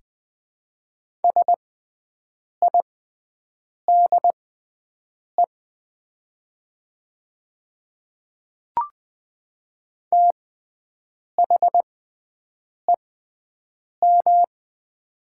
Morse code, heard as a keyed tone, about 20 wpm, spaced as for 7 wpm, 700 Hz. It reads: SIDE THEM
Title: 6x Character Spacing - Top 100 Words - 20wpm